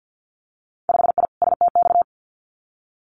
Morse code, heard as a keyed tone, about 50 wpm, 700 Hz. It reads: 5I STX
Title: Winter Field Day Exchanges Encore 50wpm